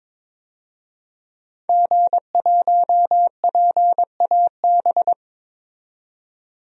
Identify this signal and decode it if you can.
Morse code, heard as a keyed tone, about 22 wpm, 700 Hz. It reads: G1PAB
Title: DXCC-100 Call Signs - 5 Characters - 22wpm